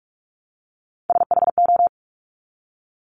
Morse code, heard as a keyed tone, about 46 wpm, 690 Hz. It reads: SHO
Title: Top Three Letter Combinations in Words 45wpm